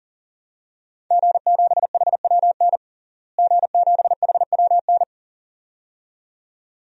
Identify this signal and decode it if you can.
Morse code, heard as a keyed tone, about 40 wpm, 700 Hz. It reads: G7HWN G7HWN
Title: DXCC-100 Call Signs - 5 Characters - Repeated 2x - 40wpm